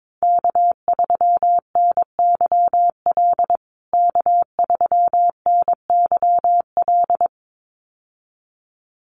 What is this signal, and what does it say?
Morse code, heard as a keyed tone, about 22 wpm, 700 Hz. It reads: K3NYL K3NYL
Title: USA Call Signs Repeated 2x 22wpm